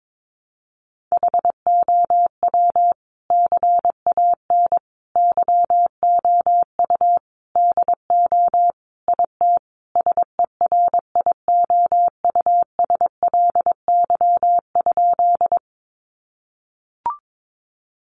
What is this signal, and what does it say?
Morse code, heard as a keyed tone, about 22 wpm, 700 Hz. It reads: HOW CAN YOU DO IT SERIOUSLY?
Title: Sentences from Top 1200 Words 22wpm